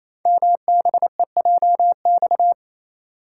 Morse code, heard as a keyed tone, about 27 wpm, 700 Hz. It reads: MBEJX
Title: Code Groups - 5 Letters - 28wpm